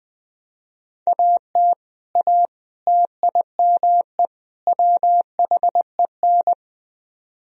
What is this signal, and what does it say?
Morse code, heard as a keyed tone, about 21 wpm, 700 Hz. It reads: AT A TIME WHEN
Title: Speed-Racing - Sets of 4 Words 17wpm